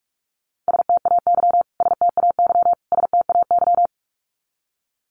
Morse code, heard as a keyed tone, about 44 wpm, 700 Hz. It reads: STAY STAY STAY
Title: Top 500 Words Repeated 45wpm